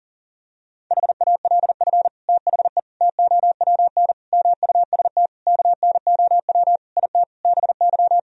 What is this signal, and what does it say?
Morse code, heard as a keyed tone, about 37 wpm, 700 Hz. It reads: HALF THE TOWN MUST KNOW IT BY
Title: Sentences from Top 1000 Words - Review Part 2 - 40wpm